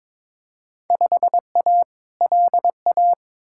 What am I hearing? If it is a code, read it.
Morse code, heard as a keyed tone, about 22 wpm, 700 Hz. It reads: 5A LA
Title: Field Day Exchanges Encore - 22wpm